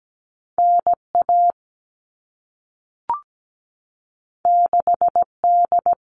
Morse code, heard as a keyed tone, about 17 wpm, 700 Hz. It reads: NA 6D
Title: All Cut Numbers - 2-Digit 17wpm